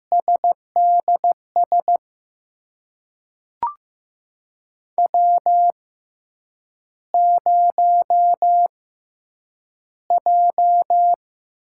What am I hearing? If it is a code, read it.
Morse code, heard as a keyed tone, about 15 wpm, spaced as for 8 wpm, 700 Hz. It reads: SDS W0J
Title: USA Call Signs - 5 Characters - 6x Character Spacing - 15wpm